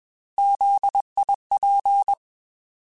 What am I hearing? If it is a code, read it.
Morse code, heard as a keyed tone, about 21 wpm, 800 Hz. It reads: ZIP
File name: Alphabetical / Alphabetical mp3s / Zs @20